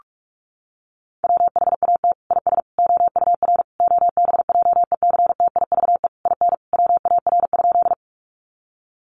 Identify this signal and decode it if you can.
Morse code, heard as a keyed tone, about 44 wpm, 700 Hz. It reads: WHAT IS OUR OBJECTIVE IN WAR?
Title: Sentences from Top 1200 Words 45wpm